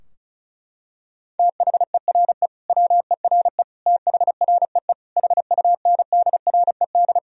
Morse code, heard as a keyed tone, about 36 wpm, 700 Hz. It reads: THERE WERE THREE HUNDRED
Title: Sentences from Top 1000 Words - Review Part 2 - 35wpm